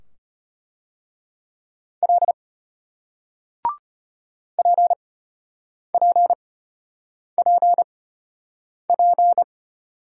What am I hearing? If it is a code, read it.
Morse code, heard as a keyed tone, about 34 wpm, spaced as for 8 wpm, 700 Hz. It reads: L PPPP